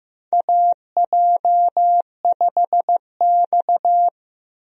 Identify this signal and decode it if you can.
Morse code, heard as a keyed tone, about 15 wpm, 700 Hz. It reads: AJ5X
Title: USA Call Signs - 4 Characters - Repeated 2x - 15wpm